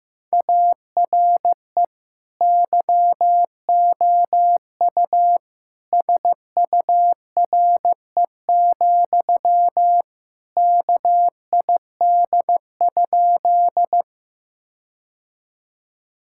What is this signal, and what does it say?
Morse code, heard as a keyed tone, about 15 wpm, 700 Hz. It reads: ARE YOU SURE, KID?
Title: Sentences from Top 1600 Words 15wpm